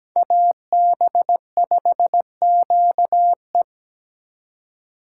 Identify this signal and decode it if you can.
Morse code, heard as a keyed tone, about 20 wpm, 700 Hz. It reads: AB5QE
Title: Speed Racing - USA 2x2 Call Signs 17wpm